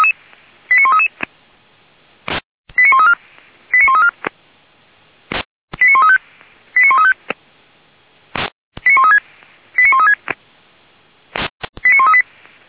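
Five-tone selective call (ZVEI), heard as quick runs of five tones, tones of about 70 ms. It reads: 89133 89134 89134 89135 89135 89136 89136 89138
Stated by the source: ZVEI five-tone call 89132 received by SDR at Feuerwehr Orlamünde (f85360).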